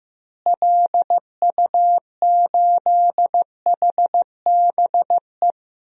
Morse code, heard as a keyed tone, about 15 wpm, 700 Hz. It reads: LU8HBE
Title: DXCC-100 Call Signs - 6 Characters - Repeated 3x - 15wpm